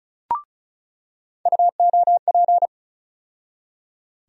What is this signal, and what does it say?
Morse code, heard as a keyed tone, about 34 wpm, 700 Hz. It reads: UOP
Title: USA Call Sign Postfixes - 3 Letters 35wpm